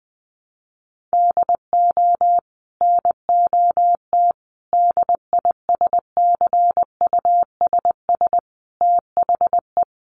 Morse code, heard as a keyed tone, about 20 wpm, 700 Hz. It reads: DO NOT DISCUSS THE
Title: Sentences from Top 1200 Words 20wpm